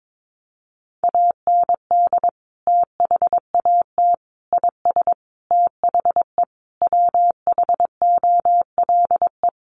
Morse code, heard as a keyed tone, about 22 wpm, 700 Hz. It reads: AND THAT IS THE WHOLE
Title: Sentences from Top 400 Words 22wpm